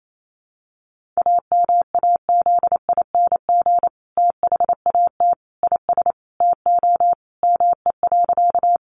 Morse code, heard as a keyed tone, about 28 wpm, 700 Hz. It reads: AMAZING THAT IS TO ME.
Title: Sentences from Top 1500 Words 28wpm